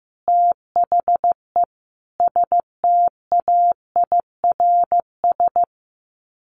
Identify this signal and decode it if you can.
Morse code, heard as a keyed tone, about 15 wpm, 700 Hz. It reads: THE STAIRS
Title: Sets of 5 Words Encore 15wpm